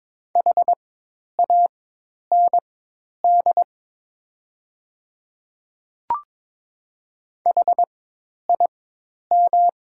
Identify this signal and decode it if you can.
Morse code, heard as a keyed tone, about 22 wpm, spaced as for 10 wpm, 700 Hz. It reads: HAND HIM
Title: 4x Character Spacing - Top 200 Words - 22wpm